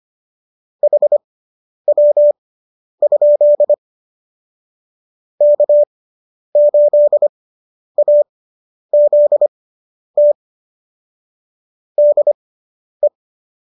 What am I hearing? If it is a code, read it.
Morse code, heard as a keyed tone, about 25 wpm, spaced as for 10 wpm, 590 Hz. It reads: HW? K8AZT DE